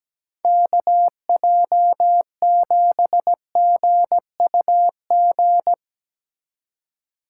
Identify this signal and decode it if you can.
Morse code, heard as a keyed tone, about 17 wpm, 700 Hz. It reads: KJ7GUG
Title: USA Call Signs - 6 Characters - 8x Character Spacing - 17wpm